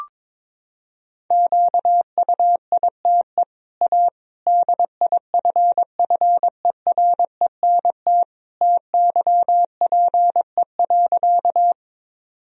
Morse code, heard as a keyed tone, about 22 wpm, 700 Hz. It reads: QUITE A DIFFERENT TYPE.
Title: Sentences from Top 1000 Words - Review Part 3 - 22wpm